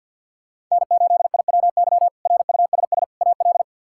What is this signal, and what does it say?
Morse code, heard as a keyed tone, about 53 wpm, 700 Hz. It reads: N8IWY RUSS AL
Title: K1USN SST QSO Practice - 50wpm